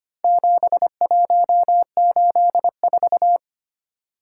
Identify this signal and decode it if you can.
Morse code, heard as a keyed tone, about 25 wpm, 700 Hz. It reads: 7184
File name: practice / 25wpm / Four Letter-Number Combinations - 25wpm